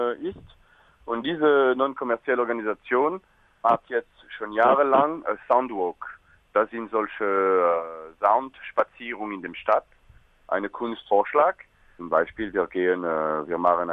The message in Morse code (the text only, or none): none